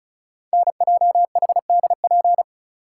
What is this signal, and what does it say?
Morse code, heard as a keyed tone, about 36 wpm, 700 Hz. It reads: NJHDP
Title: Code Groups - 5 Letters - 35wpm